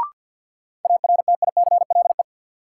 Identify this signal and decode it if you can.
Morse code, heard as a keyed tone, about 51 wpm, 700 Hz. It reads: ARTICLE
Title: Sentences from Top 1000 Words - Review Part 5 - New Word Drill - 50wpm